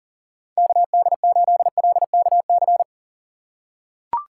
Morse code, heard as a keyed tone, about 40 wpm, 700 Hz. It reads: KD8LKC
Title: USA Call Signs - 6 Characters - 4x Character Spacing - 40wpm